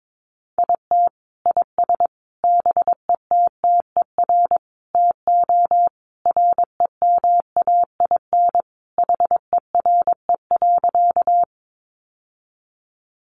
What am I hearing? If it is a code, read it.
Morse code, heard as a keyed tone, about 22 wpm, 700 Hz. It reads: IT IS BETTER TO REMAIN HERE.
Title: Sentences from Top 1200 Words 22wpm